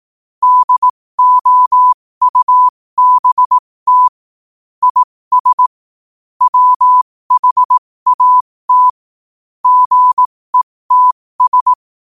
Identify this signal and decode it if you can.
Morse code, heard as a keyed tone, about 18 wpm, spaced as for 15 wpm, 1000 Hz. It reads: DOUBT IS WHAT GETS
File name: qotdfiles / 15wpm